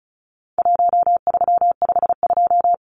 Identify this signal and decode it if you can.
Morse code, heard as a keyed tone, about 34 wpm, 700 Hz. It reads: 1352
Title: Numbers - Contest Style - 4-Digit 35wpm